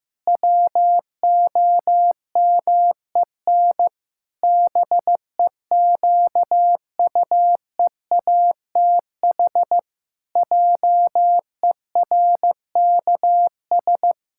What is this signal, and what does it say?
Morse code, heard as a keyed tone, about 16 wpm, 700 Hz. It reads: WOMEN BEQUEATH JERKS
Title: Pangram Challenge v2 - 15wpm